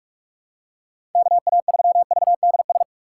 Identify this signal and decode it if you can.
Morse code, heard as a keyed tone, about 46 wpm, 700 Hz. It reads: KA3VDS
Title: USA Call Signs - 6 Characters - 6x Character Spacing - 45wpm